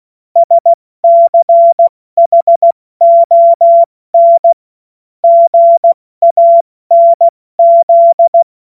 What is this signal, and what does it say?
Morse code, heard as a keyed tone, about 16 wpm, 680 Hz. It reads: SCHON GANZ
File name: Morsekurs / Witze / Witz 17. Januar 2023 70 F80